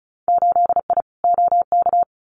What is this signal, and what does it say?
Morse code, heard as a keyed tone, about 34 wpm, 700 Hz. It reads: 8I OK
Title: Winter Field Day Exchanges Encore 35wpm